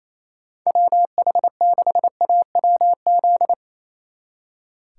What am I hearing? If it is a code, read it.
Morse code, heard as a keyed tone, about 34 wpm, 700 Hz. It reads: WH6AWZ
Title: Speed Racing - USA 2x3 Call Signs 28wpm